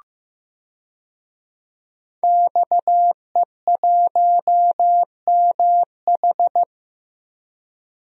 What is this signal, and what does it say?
Morse code, heard as a keyed tone, about 15 wpm, 700 Hz. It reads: XE1MH
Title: DXCC-100 Call Signs - 5 Characters - Repeated 3x - 15wpm